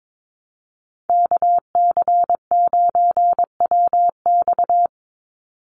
Morse code, heard as a keyed tone, about 22 wpm, 700 Hz. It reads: KC9WX
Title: USA Call Signs 22wpm